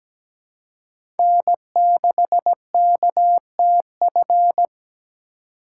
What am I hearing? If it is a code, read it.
Morse code, heard as a keyed tone, about 17 wpm, 700 Hz. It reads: N6KTF
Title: USA Call Signs - 5 Characters - 4x Character Spacing - 17wpm